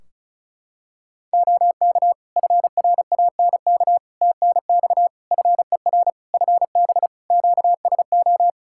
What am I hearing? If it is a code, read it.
Morse code, heard as a keyed tone, about 34 wpm, 700 Hz. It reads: OK FRANK TNX FER FB QSO
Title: Practice QSO - Exchange 4 - Ending - 35wpm